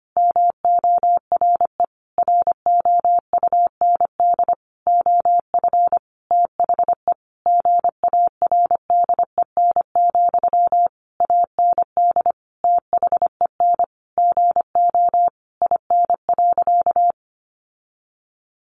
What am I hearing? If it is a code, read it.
Morse code, heard as a keyed tone, about 25 wpm, 700 Hz. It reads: MORE ROUND OF THE GARDEN, AND THEN GO IN.
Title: Sentences from Top 700 Words 25wpm